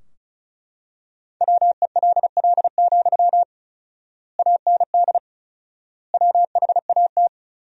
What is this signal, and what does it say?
Morse code, heard as a keyed tone, about 36 wpm, 700 Hz. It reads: WELL, AND WHAT
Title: 4x Word-Spacing - Sentences from Top 200 Words - 35wpm